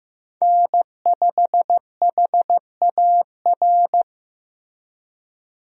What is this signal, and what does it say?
Morse code, heard as a keyed tone, about 15 wpm, 700 Hz. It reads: N5HAR
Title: USA Call Signs - 5 Characters - Repeated 3x - 15wpm